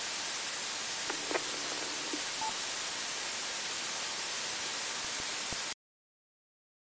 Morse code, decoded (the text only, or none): none